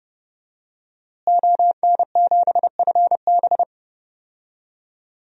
Morse code, heard as a keyed tone, about 30 wpm, 700 Hz. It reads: ON7FB
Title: DXCC-100 Call Signs - 5 Characters - Repeated 3x - 30wpm